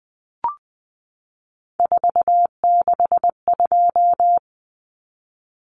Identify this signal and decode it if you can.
Morse code, heard as a keyed tone, about 20 wpm, 700 Hz. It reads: 462
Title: Common Cut Numbers - 3-Digit 20wpm